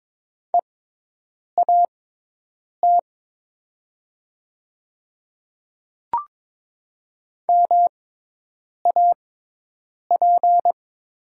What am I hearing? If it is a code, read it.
Morse code, heard as a keyed tone, about 22 wpm, spaced as for 8 wpm, 700 Hz. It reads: EAT MAP
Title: 6x Character Spacing - Top 500 Words - 22wpm